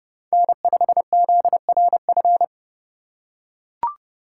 Morse code, heard as a keyed tone, about 29 wpm, 700 Hz. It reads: N5ZRF